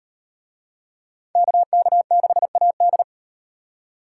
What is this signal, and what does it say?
Morse code, heard as a keyed tone, about 38 wpm, 700 Hz. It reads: KK6AD